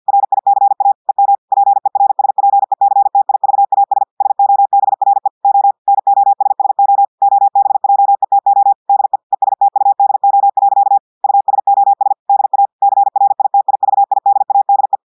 Morse code, heard as a keyed tone, about 50 wpm, 800 Hz. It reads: RICA EM PERSPECTIVAS SOBRE O NOSSO OBJETO DE ESTUDO. USOS DA CRITIVIDADE
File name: audios / portugais / portugais texte en clair 1 vitesse 50 mots minutes